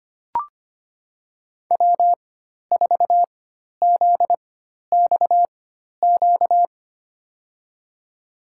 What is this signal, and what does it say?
Morse code, heard as a keyed tone, about 25 wpm, spaced as for 12 wpm, 700 Hz. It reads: W4ZXQ
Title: USA Call Signs - 5 Characters - 4x Character Spacing - 25wpm